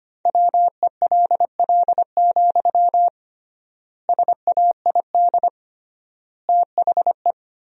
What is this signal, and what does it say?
Morse code, heard as a keyed tone, about 25 wpm, 700 Hz. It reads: WELL, SAID THE
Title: 3x Word-Spacing - Sentences from Top 400 Words - 25wpm